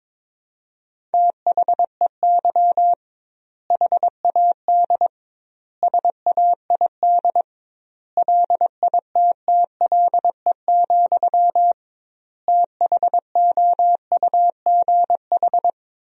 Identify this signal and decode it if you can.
Morse code, heard as a keyed tone, about 22 wpm, 700 Hz. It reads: THEY HAD SAID LITTLE, THOUGH
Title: 2x Word-Spacing - Sentences from Top 400 Words - 22wpm